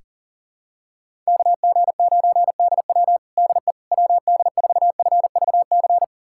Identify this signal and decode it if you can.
Morse code, heard as a keyed tone, about 40 wpm, 700 Hz. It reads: KG9DW DE WD4FVC
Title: USA Callsign DE Callsign 40wpm